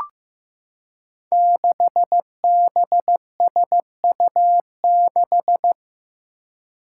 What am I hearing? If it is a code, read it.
Morse code, heard as a keyed tone, about 15 wpm, 700 Hz. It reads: 6BSU6
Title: Five Challenging Letter-Number Combinations 15wpm